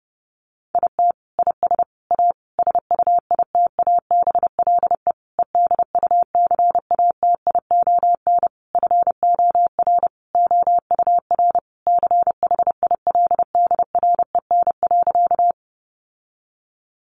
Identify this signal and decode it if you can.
Morse code, heard as a keyed tone, about 30 wpm, 700 Hz. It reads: IT IS A SUITABLE EDUCATION FOR OUR CHILDREN.
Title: Sentences from Top 1400 Words 30wpm